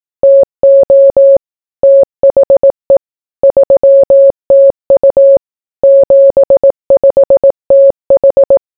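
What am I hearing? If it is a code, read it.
Morse code, heard as a keyed tone, about 18 wpm, 550 Hz. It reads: TO THE 3TU 75TH
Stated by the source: Morse code welcome for a 75th anniversary reunion dinner of a telecommunications unit.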